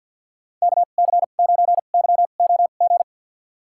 Morse code, heard as a keyed tone, about 46 wpm, spaced as for 35 wpm, 700 Hz. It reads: KC9YOG
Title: USA Call Signs - 2x Character Spacing - 50wpm